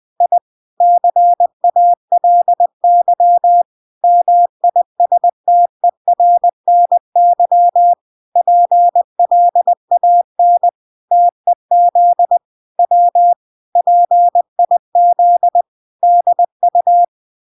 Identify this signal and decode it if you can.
Morse code, heard as a keyed tone, about 20 wpm, 700 Hz. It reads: I CALY MISTERNY PLAN TEZ W PIZDU